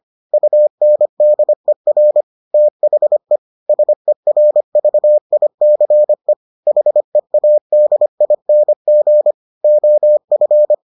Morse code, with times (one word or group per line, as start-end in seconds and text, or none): 0.3-2.2 UNDER
2.5-3.4 THE
3.7-6.3 SERVICE
6.7-9.3 HEADING
9.6-10.7 OF